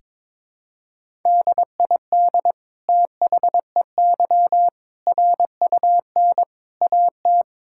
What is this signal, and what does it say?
Morse code, heard as a keyed tone, about 22 wpm, 700 Hz. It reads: DID THEY RUN AT
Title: Sentences from Top 1000 Words - Review Part 5 - 22wpm